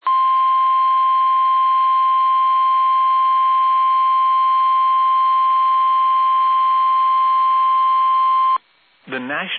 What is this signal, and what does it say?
Emergency Alert System (SAME) tones (the 1050 Hz attention tone), with speech. It is heard from 0.0 to 8.6 s.